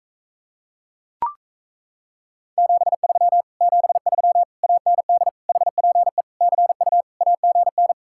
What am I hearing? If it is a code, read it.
Morse code, heard as a keyed tone, about 38 wpm, 700 Hz. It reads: 73 73 AND HPE CU AGN